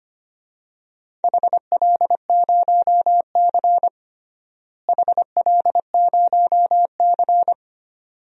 Morse code, heard as a keyed tone, about 25 wpm, 700 Hz. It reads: HL0C HL0C
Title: DXCC-100 Call Signs - 4 Characters - Repeated 2x - 25wpm